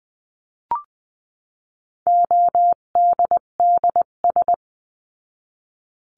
Morse code, heard as a keyed tone, about 20 wpm, 700 Hz. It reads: ODDS
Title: 258 Lesson 42c - Words 16-20wpm